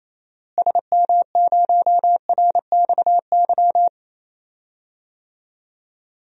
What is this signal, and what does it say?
Morse code, heard as a keyed tone, about 27 wpm, 700 Hz. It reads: SM0RXY
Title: DXCC-100 Call Signs - 6 Characters - Repeated 3x - 28wpm